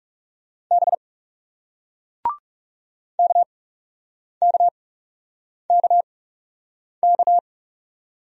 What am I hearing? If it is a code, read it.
Morse code, heard as a keyed tone, about 38 wpm, spaced as for 8 wpm, 700 Hz. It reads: B KKKK